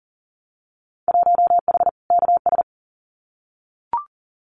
Morse code, heard as a keyed tone, about 41 wpm, 700 Hz. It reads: 1H KS
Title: Winter Field Day Exchanges 40wpm